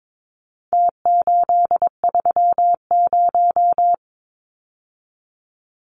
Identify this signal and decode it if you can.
Morse code, heard as a keyed tone, about 22 wpm, 700 Hz. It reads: T830
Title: Numbers - Contest Style - 4-Digit 22wpm